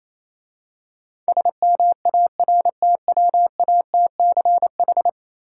Morse code, heard as a keyed tone, about 27 wpm, 700 Hz. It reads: SMARTWATCH
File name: Easy Words - Speed-Racing with Hints - 22wpm